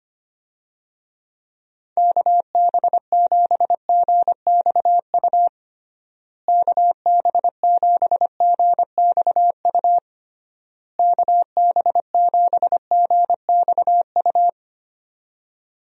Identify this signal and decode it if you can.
Morse code, heard as a keyed tone, about 25 wpm, 700 Hz. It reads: KB7GXU KB7GXU KB7GXU
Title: USA Call Signs - 6 Characters - Repeated 3x - 25wpm